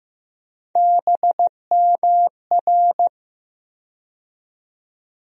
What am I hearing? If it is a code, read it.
Morse code, heard as a keyed tone, about 15 wpm, 700 Hz. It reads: BMR
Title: USA Call Sign Postfixes - 3 Letters 15wpm